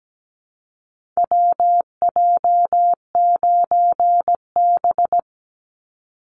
Morse code, heard as a keyed tone, about 17 wpm, 700 Hz. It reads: WJ9B
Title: Call Signs - 17wpm